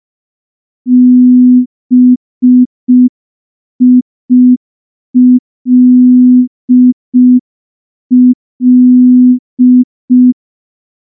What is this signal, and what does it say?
Morse code, heard as a keyed tone, about 5 wpm, 250 Hz. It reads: BILL
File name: morse / morse-training-05WPM